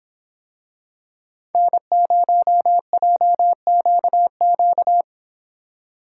Morse code, heard as a keyed tone, about 25 wpm, 700 Hz. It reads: N0JQQ